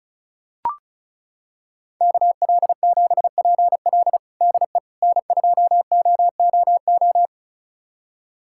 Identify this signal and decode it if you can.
Morse code, heard as a keyed tone, about 34 wpm, 700 Hz. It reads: KL7PL DE N2OOO
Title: USA Callsign DE Callsign - 5 Characters - 35wpm